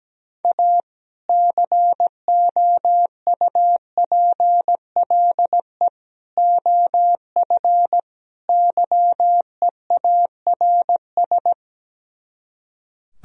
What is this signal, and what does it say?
Morse code, heard as a keyed tone, about 17 wpm, 700 Hz. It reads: A COUPLE OF YEARS
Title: Speed-Racing - Sets of 4 Words 17wpm